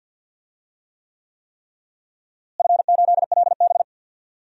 Morse code, heard as a keyed tone, about 52 wpm, 700 Hz. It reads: F8LB
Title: DXCC-100 Call Signs - 4 Characters - Repeated 3x - 50wpm